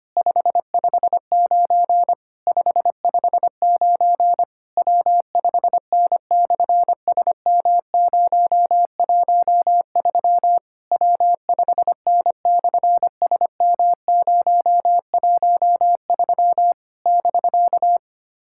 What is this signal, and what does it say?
Morse code, heard as a keyed tone, about 24 wpm, 700 Hz. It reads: 559 559 W5N/SM013 W5N/SM013 <BK>
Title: SOTA QSO Practice 25wpm